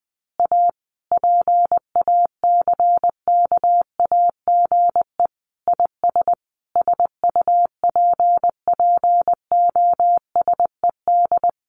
Morse code, heard as a keyed tone, about 20 wpm, 700 Hz. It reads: A PACKAGE IS SUPPOSED
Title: Additional Easy Phrases 20wpm